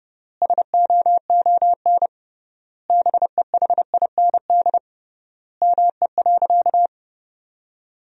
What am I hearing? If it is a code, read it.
Morse code, heard as a keyed tone, about 30 wpm, 700 Hz. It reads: SOON BEHIND ME.
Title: 3x Word-Spacing - Sentences from Top 500 Words - 30wpm